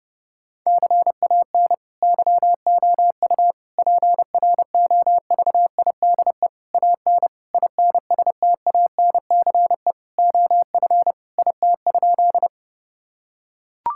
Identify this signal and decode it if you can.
Morse code, heard as a keyed tone, about 30 wpm, 700 Hz. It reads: CAN YOU PROVIDE AN INSTANCE OF IT?
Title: Sentences from Top 1000 Words - Review Part 5 - 30wpm